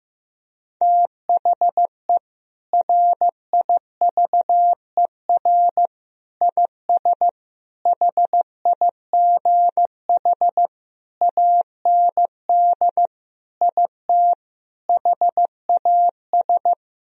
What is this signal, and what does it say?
Morse code, heard as a keyed tone, about 14 wpm, 700 Hz. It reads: THE RIVER IS HIGH AND IT HAS
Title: Sentences from Top 1000 Words - Review Part 4 - 15wpm